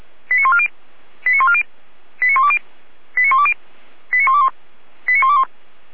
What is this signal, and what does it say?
Five-tone selective call (ZVEI), heard as quick runs of five tones, tones of about 70 ms. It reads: 89140 89140 89120 89120 89121 89121